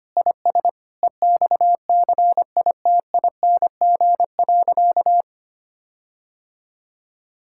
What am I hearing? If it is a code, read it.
Morse code, heard as a keyed tone, about 25 wpm, 700 Hz. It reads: IS EXCITING.
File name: practice / 25wpm / Sentences with 4-Word Phrases - 25wpm